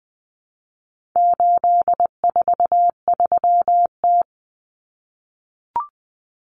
Morse code, heard as a keyed tone, about 20 wpm, 700 Hz. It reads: 843T
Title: Common Cut Numbers - 4-Digit 20wpm